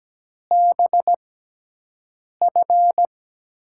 Morse code, heard as a keyed tone, about 17 wpm, spaced as for 6 wpm, 700 Hz. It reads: BF
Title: USA Call Signs - 6 Characters - 6x Character Spacing - 17wpm